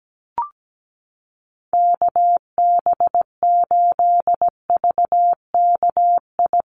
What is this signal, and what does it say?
Morse code, heard as a keyed tone, about 17 wpm, 700 Hz. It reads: KB8VKI
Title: USA Call Signs Repeated 3x 17wpm